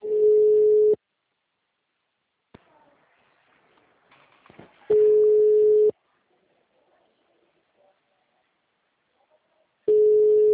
Ringback tone (Europe, 425 Hz).